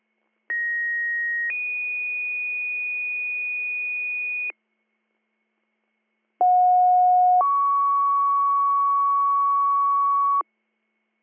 Two-tone page, heard as a one-second tone, then a three-second tone, about 1900 then 2450 Hz, about 730 then 1120 Hz. It is heard from 0.5 to 10.4 s.